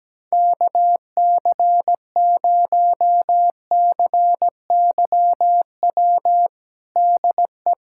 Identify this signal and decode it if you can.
Morse code, heard as a keyed tone, about 17 wpm, 700 Hz. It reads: KC0CYW DE
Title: USA Callsign DE Callsign - 6 Characters - 17wpm